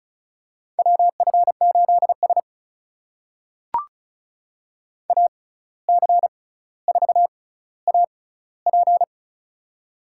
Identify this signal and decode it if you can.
Morse code, heard as a keyed tone, about 36 wpm, spaced as for 17 wpm, 700 Hz. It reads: WF8S AC4AP
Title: USA Call Signs - 6x Character Spacing - 35wpm